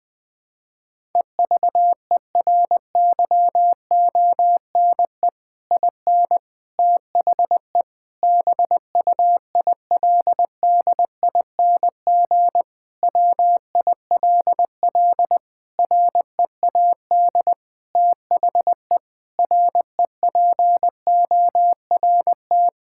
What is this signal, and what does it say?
Morse code, heard as a keyed tone, about 21 wpm, 700 Hz. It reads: EVERYONE IN THE BUILDING WILL READ THE REPORT
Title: Sentences from Top 1000 Words - Review Part 6 - 20wpm